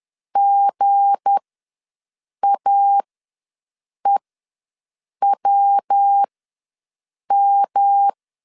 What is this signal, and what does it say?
Morse code, heard as a keyed tone, about 10 wpm, spaced as for 6 wpm, 800 Hz. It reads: GAEWM
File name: LEZIONI / 9 286 ws 10wpm 4032